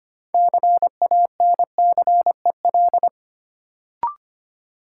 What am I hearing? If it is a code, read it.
Morse code, heard as a keyed tone, about 25 wpm, 700 Hz. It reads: CANCEL
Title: Sentences from Top 1200 Words - New Word Drill - 25wpm